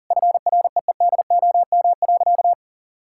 Morse code, heard as a keyed tone, about 40 wpm, 700 Hz. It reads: FREEDOM.
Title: Sentences from Top 1000 Words - Review Part 5 - 40wpm